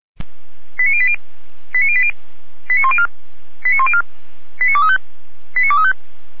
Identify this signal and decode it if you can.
Five-tone selective call (ZVEI), heard as quick runs of five tones, tones of about 70 ms. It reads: 89080 89080 89104 89104 89235 89235